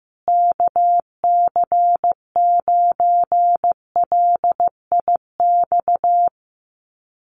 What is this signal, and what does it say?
Morse code, heard as a keyed tone, about 15 wpm, 700 Hz. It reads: KC9LIX
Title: USA Call Signs Repeated 3x 15wpm